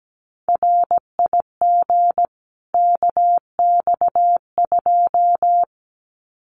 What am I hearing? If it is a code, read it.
Morse code, heard as a keyed tone, about 17 wpm, 700 Hz. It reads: RIG KX2
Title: Common QSO Phrases 17wpm